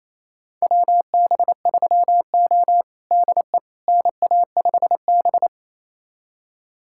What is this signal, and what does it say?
Morse code, heard as a keyed tone, about 28 wpm, 700 Hz. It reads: WB3O DE NA5B
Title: USA Callsign DE Callsign - 4 Characters - 28wpm